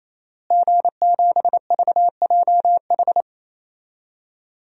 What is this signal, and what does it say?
Morse code, heard as a keyed tone, about 28 wpm, 690 Hz. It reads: G7VJH
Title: DXCC-100 Call Signs - 5 Characters - Repeated 3x - 28wpm